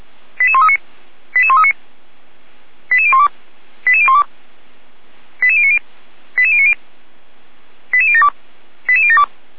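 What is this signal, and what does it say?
Five-tone selective call (ZVEI), heard as quick runs of five tones, tones of about 70 ms. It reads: 88139 88139 88012 88012 88090 88090 88072 88072
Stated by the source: Five-tone ZVEI selective call received by SDR, fire brigade Orlamünde (feuerwehr-orlamuende).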